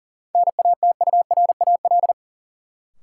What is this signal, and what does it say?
Morse code, heard as a keyed tone, about 40 wpm, 700 Hz. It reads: NATURAL
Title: Sentences from Top 700 Words - New Word Drill - 40wpm